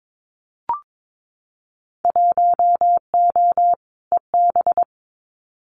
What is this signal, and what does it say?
Morse code, heard as a keyed tone, about 22 wpm, 700 Hz. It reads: 1O EB
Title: Winter Field Day Exchanges 22wpm